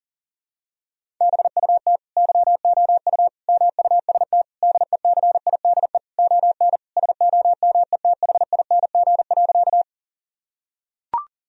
Morse code, heard as a keyed tone, about 40 wpm, 700 Hz. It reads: BUT YOU MUST DECIDE ON SOMETHING.
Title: Sentences from Top 1000 Words - Review Part 3 - 40wpm